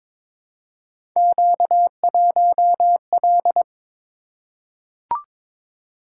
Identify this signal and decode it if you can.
Morse code, heard as a keyed tone, about 22 wpm, 700 Hz. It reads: Q1L
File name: practice / 22wpm / Three Letter-Number Combinations - 22wpm